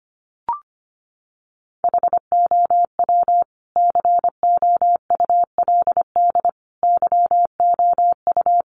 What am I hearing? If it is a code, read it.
Morse code, heard as a keyed tone, about 25 wpm, 700 Hz. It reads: HOW COULD YOU
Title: Sentences from Top 1100 Words 25wpm